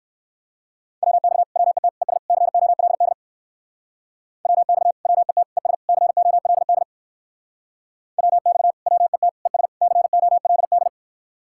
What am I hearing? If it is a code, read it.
Morse code, heard as a keyed tone, about 58 wpm, 700 Hz. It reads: WX WET ES COLD WX WET ES COLD WX WET ES COLD